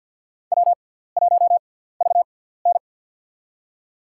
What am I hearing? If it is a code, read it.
Morse code, heard as a keyed tone, about 52 wpm, spaced as for 17 wpm, 700 Hz. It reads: W1VN
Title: USA Call Signs - 4 Characters - 6x Character Spacing - 50wpm